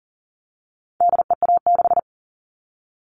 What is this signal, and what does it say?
Morse code, heard as a keyed tone, about 40 wpm, 700 Hz. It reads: DEA6
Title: All Cut Numbers - 4-Digit 40wpm